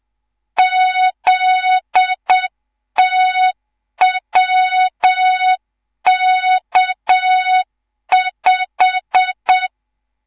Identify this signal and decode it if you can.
Morse code, heard as a keyed tone, about 7 wpm, 3700 Hz. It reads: ZTWK5
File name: Morze-HA2MN / hang / 11 GYAKORLAT